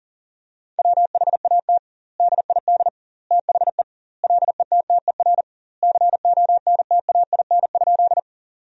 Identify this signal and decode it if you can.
Morse code, heard as a keyed tone, about 37 wpm, 700 Hz. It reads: WHAT DID THE LETTER CONTAIN?
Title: 2x Word-Spacing - Sentences from Top 500 Words - 40wpm